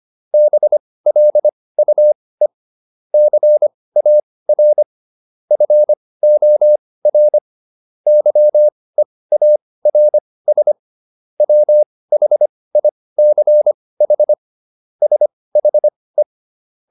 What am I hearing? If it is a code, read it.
Morse code, heard as a keyed tone, about 25 wpm, spaced as for 18 wpm, 600 Hz. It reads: BLUE CAR FOR YEARS WHICH SHE